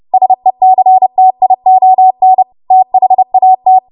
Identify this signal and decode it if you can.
Morse code, heard as a keyed tone, about 30 wpm, 750 Hz. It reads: SECTION THAT
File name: Morse / 30WPM